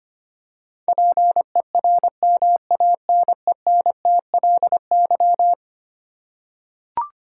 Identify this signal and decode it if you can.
Morse code, heard as a keyed tone, about 25 wpm, 700 Hz. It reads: PERMANENTLY
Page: morse-code-ninja-practice-25wpm